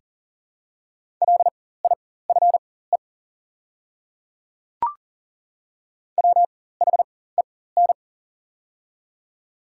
Morse code, heard as a keyed tone, about 40 wpm, spaced as for 19 wpm, 700 Hz. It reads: LIFE WHEN
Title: 4x Character Spacing - Top 500 Words - 40wpm